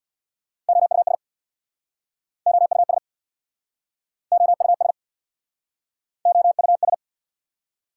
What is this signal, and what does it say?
Morse code, heard as a keyed tone, about 69 wpm, 690 Hz. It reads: OUS OUS OUS OUS